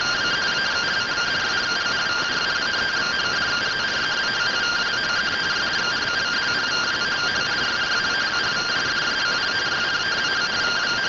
Two-tone teleprinter signal, about 170 Hz shift.